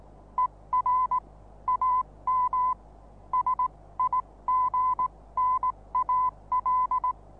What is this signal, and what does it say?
Morse code, heard as a keyed tone, about 19 wpm, spaced as for 16 wpm, 1010 Hz. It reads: ER AM SIGNAL
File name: repos / Morsekurs / DJ4UF Morsen26